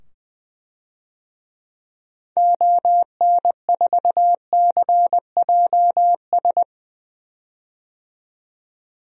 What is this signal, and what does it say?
Morse code, heard as a keyed tone, about 20 wpm, 700 Hz. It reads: ON4CJS
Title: DXCC-100 Call Signs - 6 Characters - 20wpm